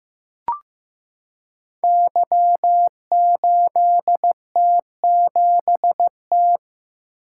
Morse code, heard as a keyed tone, about 15 wpm, 700 Hz. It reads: Y8T7T
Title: Code Groups - 5 Letter:Numbers - 15wpm